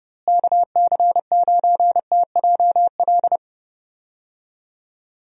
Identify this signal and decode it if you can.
Morse code, heard as a keyed tone, about 30 wpm, 700 Hz. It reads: KC9TJL
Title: USA Call Signs - 6 Characters - Repeated 3x - 30wpm